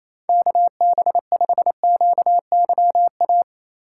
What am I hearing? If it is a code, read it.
Morse code, heard as a keyed tone, about 27 wpm, 700 Hz. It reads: KB5QYA
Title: USA Call Signs - 6 Characters - 4x Character Spacing - 28wpm